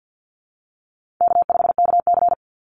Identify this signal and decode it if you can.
Morse code, heard as a keyed tone, about 50 wpm, 700 Hz. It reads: K5KC
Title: Easy Call Signs - 50wpm